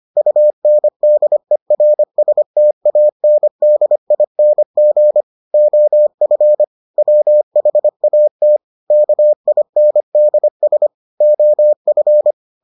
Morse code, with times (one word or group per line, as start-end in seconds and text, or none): 0.2-5.2 UNDERSTANDING
5.5-6.6 OF
7.0-8.6 WHAT
8.9-10.9 KINDS
11.2-12.3 OF